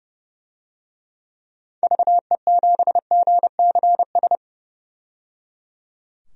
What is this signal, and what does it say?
Morse code, heard as a keyed tone, about 30 wpm, 700 Hz. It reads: VE7GCS
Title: DXCC-100 Call Signs - 6 Characters - 30wpm